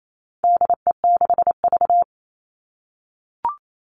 Morse code, heard as a keyed tone, about 28 wpm, 700 Hz. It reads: DE6V
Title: All Cut Numbers - 4-Digit 28wpm